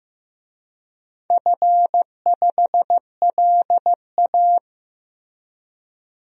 Morse code, heard as a keyed tone, about 15 wpm, 700 Hz. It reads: F5LA
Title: DXCC-100 Call Signs - 4 Characters - Repeated 3x - 15wpm